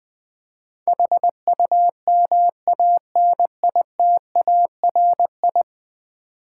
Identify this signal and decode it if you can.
Morse code, heard as a keyed tone, about 20 wpm, 700 Hz. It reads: HUMANITARI
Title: Build A Word - 5 Letters At A Time - 12-Letter Words - 20wpm